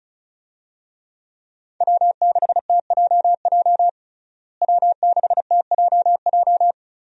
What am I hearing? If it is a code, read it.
Morse code, heard as a keyed tone, about 34 wpm, 700 Hz. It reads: W6TJJ W6TJJ
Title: USA Call Signs - 5 Characters - Repeated 3x - 35wpm